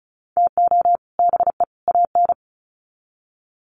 Morse code, heard as a keyed tone, about 34 wpm, 700 Hz. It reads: TO BE AN
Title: Sets of 3 Words 35wpm